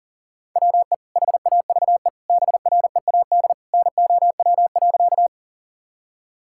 Morse code, heard as a keyed tone, about 40 wpm, 700 Hz. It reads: WE HAVE BREAD NOW.
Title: Sentences from Top 1600 Words 40wpm